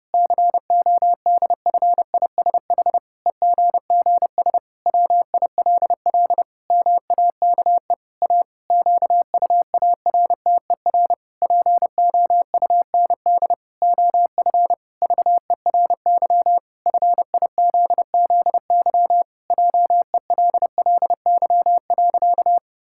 Morse code, heard as a keyed tone, about 30 wpm, 700 Hz. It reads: CODFISH EGGS WILL MAKE A QUARTER POUND OF VERY FIZZY JELLY.
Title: Pangram Challenge v2 - 30wpm